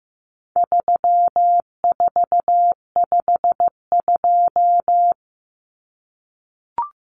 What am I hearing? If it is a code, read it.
Morse code, heard as a keyed tone, about 14 wpm, 700 Hz. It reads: 3452
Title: Numbers - 4-Digit 15wpm